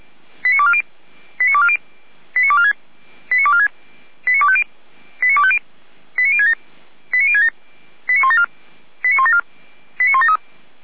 Five-tone selective call (ZVEI), heard as quick runs of five tones, tones of about 70 ms. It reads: 89240 89240 89246 89246 89250 89250 89067 89067 89174 89174 89173
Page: /455/SDR/ZVEI/89160/